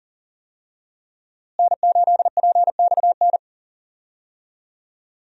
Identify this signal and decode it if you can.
Morse code, heard as a keyed tone, about 40 wpm, 700 Hz. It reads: N8PXN
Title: USA Call Signs - 5 Characters - Repeated 3x - 40wpm